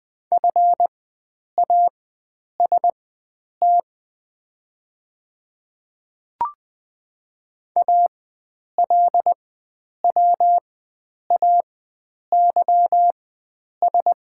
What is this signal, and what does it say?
Morse code, heard as a keyed tone, about 20 wpm, spaced as for 9 wpm, 700 Hz. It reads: FAST ALWAYS
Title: 4x Character Spacing - Top 500 Words - 20wpm